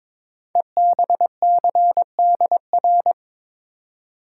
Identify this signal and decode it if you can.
Morse code, heard as a keyed tone, about 22 wpm, 700 Hz. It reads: EBCDR
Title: Code Groups - 5 Letters - 22wpm